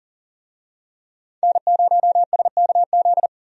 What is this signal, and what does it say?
Morse code, heard as a keyed tone, about 40 wpm, 700 Hz. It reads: N0SKZ
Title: USA Call Signs - 5 Characters - Repeated 3x - 40wpm